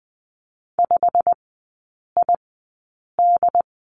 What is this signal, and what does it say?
Morse code, heard as a keyed tone, about 20 wpm, spaced as for 8 wpm, 690 Hz. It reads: 5ID
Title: 087 Lesson 15e - Number 5 - Three Characters 20wpm